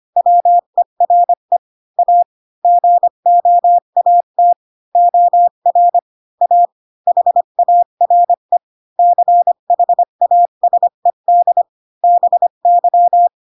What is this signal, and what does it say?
Morse code, heard as a keyed tone, about 25 wpm, 700 Hz. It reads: WERE A GOAT OR A HARE CHASED BY
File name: Pinocchio / Pinocchio ch4 20 25 ik1hge